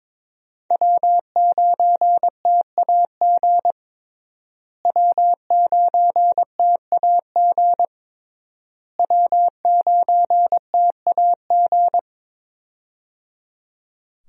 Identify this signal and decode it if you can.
Morse code, heard as a keyed tone, about 21 wpm, 700 Hz. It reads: W9TAG W9TAG W9TAG
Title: USA Call Signs - 5 Characters - Repeated 3x - 22wpm